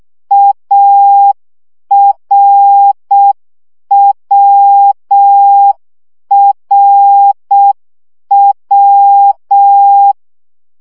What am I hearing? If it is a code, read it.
Morse code, heard as a keyed tone, about 6 wpm, 800 Hz. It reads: ARWRW
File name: CW gyakorlas a r w 5 baud